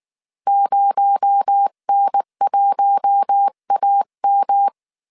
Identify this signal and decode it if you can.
Morse code, heard as a keyed tone, about 19 wpm, 800 Hz. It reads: 0N1AM
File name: CW / DET / 1 9 1500C S C 18wpm 2539